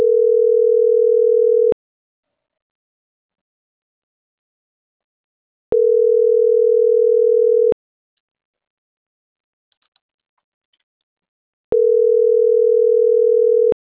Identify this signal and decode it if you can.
Ringback tone (North America), 440 + 480 Hz.
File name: CENTRAL MAYAB / out fernanda.franco